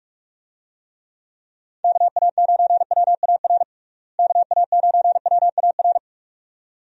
Morse code, heard as a keyed tone, about 46 wpm, 700 Hz. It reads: KA9WAR KA9WAR
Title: USA Call Signs - 6 Characters - Repeated 2x - 45wpm